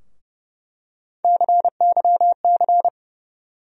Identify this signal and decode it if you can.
Morse code, heard as a keyed tone, about 30 wpm, 700 Hz. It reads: CYC
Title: USA Call Sign Postfixes - 3 Letters 30wpm